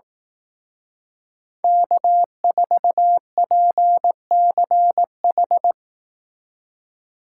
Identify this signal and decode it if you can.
Morse code, heard as a keyed tone, about 18 wpm, 700 Hz. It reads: K4PCH
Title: Speed Racing - USA 1x3 Call Signs 15wpm